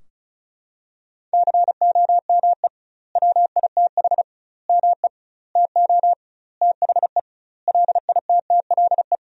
Morse code, heard as a keyed tone, about 36 wpm, 700 Hz. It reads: COME WITH ME TO THE LITTLE
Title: 2x Word-Spacing - Sentences from Top 200 Words - 35wpm